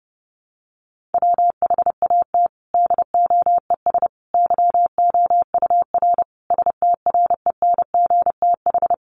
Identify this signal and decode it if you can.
Morse code, heard as a keyed tone, about 30 wpm, 700 Hz. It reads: WHAT DOES YOUR STRENGTH
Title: Sentences from Top 1400 Words 30wpm